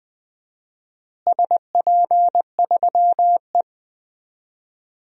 Morse code, heard as a keyed tone, about 20 wpm, 700 Hz. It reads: SP3E
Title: DXCC-100 Call Signs - 4 Characters - Repeated 3x - 20wpm